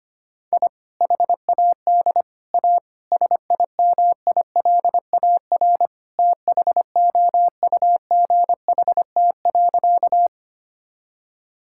Morse code, heard as a keyed tone, about 25 wpm, 700 Hz. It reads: I HAD A SIMILAR THOUGHT.
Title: Sentences from Top 1000 Words - Review Part 5 - 25wpm